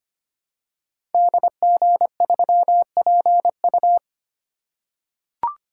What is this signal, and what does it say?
Morse code, heard as a keyed tone, about 25 wpm, 700 Hz. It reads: DG3PU
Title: Code Groups - 5 Letter:Numbers - 25wpm